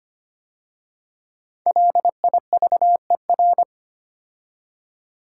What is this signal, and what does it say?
Morse code, heard as a keyed tone, about 25 wpm, 700 Hz. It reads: LIVER
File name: 5 Letter Words - First 3 Letters Given - 25wpm